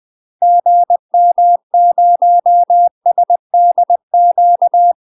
Morse code, heard as a keyed tone, about 20 wpm, 700 Hz. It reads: GM0SDQ
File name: UK MASTER QSOs TRAINING / 20WPM / UK MASTER QSOs 0060